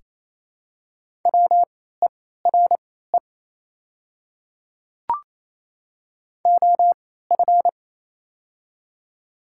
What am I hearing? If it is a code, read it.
Morse code, heard as a keyed tone, about 28 wpm, spaced as for 16 wpm, 700 Hz. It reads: WERE OF
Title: 3x Character Spacing - Top 200 Words - 28wpm